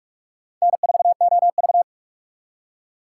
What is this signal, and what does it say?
Morse code, heard as a keyed tone, about 55 wpm, 700 Hz. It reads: N4OV
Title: Speed Racing - USA Call Signs 45wpm